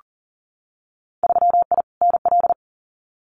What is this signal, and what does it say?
Morse code, heard as a keyed tone, about 40 wpm, 700 Hz. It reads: 3I NL
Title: Winter Field Day Exchanges 40wpm